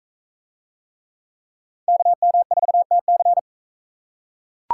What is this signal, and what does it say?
Morse code, heard as a keyed tone, about 42 wpm, 700 Hz. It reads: KM4TC